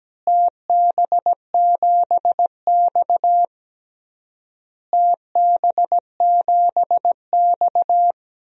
Morse code, heard as a keyed tone, about 17 wpm, 700 Hz. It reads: TB7X TB7X
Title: DXCC-100 Call Signs - 4 Characters - Repeated 3x - 17wpm